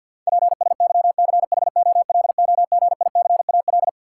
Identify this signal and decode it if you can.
Morse code, heard as a keyed tone, about 53 wpm, 700 Hz. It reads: PSYCHOLOGICAL
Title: Sentences from Top 1400 Words - New Word Drill - 50wpm